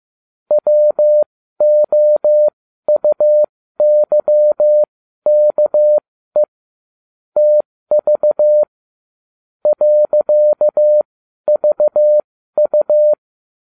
Morse code, heard as a keyed tone, about 15 wpm, spaced as for 12 wpm, 600 Hz. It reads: WOUYKE TV .VU